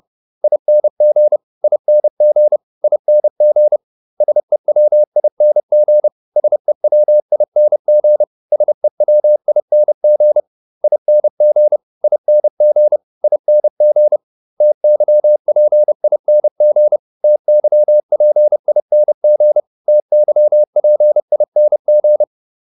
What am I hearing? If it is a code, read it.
Morse code, heard as a keyed tone, about 30 wpm, 600 Hz. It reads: ING ING ING SEWING SEWING SEWING ING ING ING TYPING TYPING TYPING